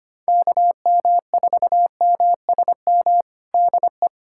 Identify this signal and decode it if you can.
Morse code, heard as a keyed tone, about 25 wpm, 700 Hz. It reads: KM4MSM DE
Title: USA Callsign DE Callsign 25wpm